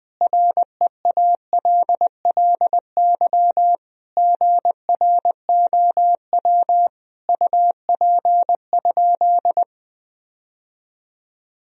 Morse code, heard as a keyed tone, about 20 wpm, 700 Hz. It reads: REALLY GROW UP?
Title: Sentences from Top 1000 Words - Review Part 1 - 20wpm